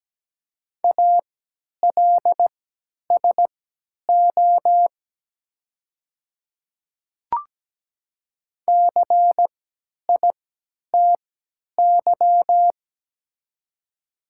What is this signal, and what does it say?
Morse code, heard as a keyed tone, about 17 wpm, spaced as for 10 wpm, 700 Hz. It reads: ALSO CITY
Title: 3x Character Spacing - Top 500 Words - 17wpm